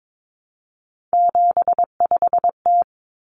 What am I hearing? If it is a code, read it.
Morse code, heard as a keyed tone, about 22 wpm, 700 Hz. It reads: 75T
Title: Common Cut Numbers - 3-Digit 22wpm